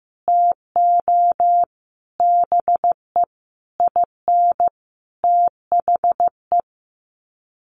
Morse code, heard as a keyed tone, about 15 wpm, 700 Hz. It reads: TO BE IN THE
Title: Sets of 5 Words Encore 15wpm